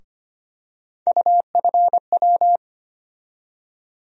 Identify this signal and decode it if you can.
Morse code, heard as a keyed tone, about 25 wpm, 700 Hz. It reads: UFW
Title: USA Call Sign Postfixes - 3 Letters 25wpm